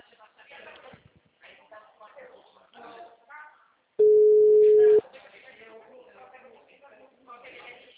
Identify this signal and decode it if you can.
Ringback tone (Europe, 425 Hz), with speech.